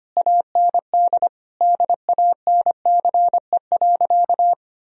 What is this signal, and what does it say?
Morse code, heard as a keyed tone, about 25 wpm, 700 Hz. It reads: AND DANCE.
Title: Sentences from Top 1000 Words - Review Part 4 - 25wpm